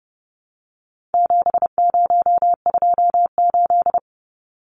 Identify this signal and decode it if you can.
Morse code, heard as a keyed tone, about 30 wpm, 700 Hz. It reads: 7028
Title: Numbers - 4-Digit 30wpm